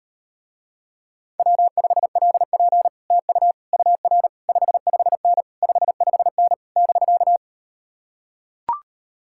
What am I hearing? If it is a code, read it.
Morse code, heard as a keyed tone, about 39 wpm, 700 Hz. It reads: W5LP TU UR 55N 55N <BK>